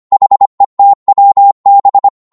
Morse code, heard as a keyed tone, about 25 wpm, 800 Hz. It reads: HETWB
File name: Morse / ltr-25wpm